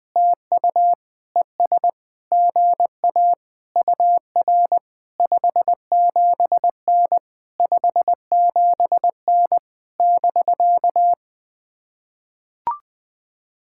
Morse code, heard as a keyed tone, about 20 wpm, 700 Hz. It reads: TU ES GA UR 57N 57N <BK>
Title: POTA QSO Practice - 20wpm